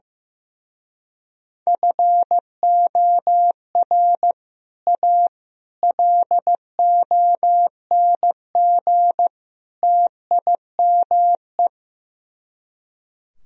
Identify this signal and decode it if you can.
Morse code, heard as a keyed tone, about 15 wpm, 700 Hz. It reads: FOR A LONG TIME
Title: Speed-Racing - Sets of 4 Words 15wpm